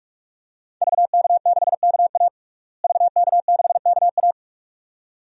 Morse code, heard as a keyed tone, about 46 wpm, 700 Hz. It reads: VK6KA VK6KA
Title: DXCC-100 Call Signs - 5 Characters - Repeated 2x - 45wpm